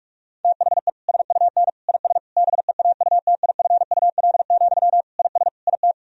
Morse code, heard as a keyed tone, about 46 wpm, 700 Hz. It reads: THE SUN IS BEAUTIFUL, IS IT
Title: Sentences from Top 1000 Words - Review Part 1 - 45wpm